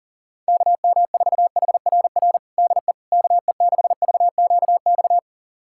Morse code, heard as a keyed tone, about 40 wpm, 700 Hz. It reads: KM4HRR DE KE6VQX
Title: USA Callsign DE Callsign 40wpm